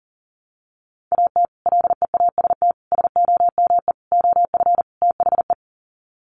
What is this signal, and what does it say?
Morse code, heard as a keyed tone, about 40 wpm, 700 Hz. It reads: AT LEAST SOME OF THE
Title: Sets of 5 Words Encore 40wpm